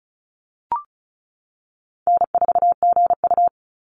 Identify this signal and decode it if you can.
Morse code, heard as a keyed tone, about 36 wpm, 700 Hz. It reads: N4GU
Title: Call Signs - 35wpm